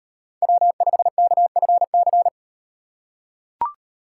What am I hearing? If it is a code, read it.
Morse code, heard as a keyed tone, about 35 wpm, 700 Hz. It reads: W5KFC